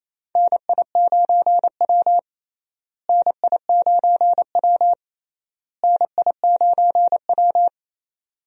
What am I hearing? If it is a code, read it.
Morse code, heard as a keyed tone, about 28 wpm, 700 Hz. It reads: NI9W NI9W NI9W
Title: USA Call Signs - 4 Characters - Repeated 3x - 28wpm